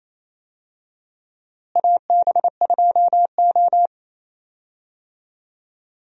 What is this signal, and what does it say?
Morse code, heard as a keyed tone, about 28 wpm, 700 Hz. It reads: AB2O
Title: USA Call Signs - 4 Characters - Repeated 3x - 28wpm